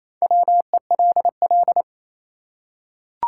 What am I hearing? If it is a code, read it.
Morse code, heard as a keyed tone, about 27 wpm, 700 Hz. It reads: WELL
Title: Top 180 QSO Elements 28wpm